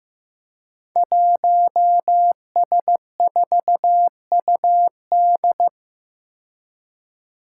Morse code, heard as a keyed tone, about 15 wpm, 700 Hz. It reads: 1S4UD
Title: Five Challenging Letter-Number Combinations 15wpm